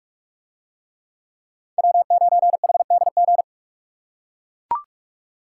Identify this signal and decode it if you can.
Morse code, heard as a keyed tone, about 42 wpm, 700 Hz. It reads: W9HDG